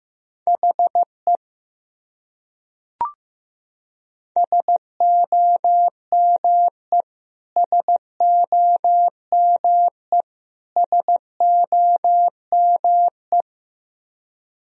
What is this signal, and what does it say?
Morse code, heard as a keyed tone, about 16 wpm, 700 Hz. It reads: HE SOME SOME SOME
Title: Top 100 Words Repeated 15wpm